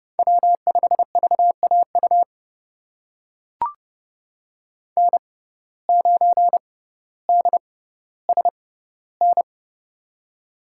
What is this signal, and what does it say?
Morse code, heard as a keyed tone, about 30 wpm, spaced as for 15 wpm, 700 Hz. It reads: W5VAU N9DSN
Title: USA Call Signs - 5 Characters - 6x Character Spacing - 30wpm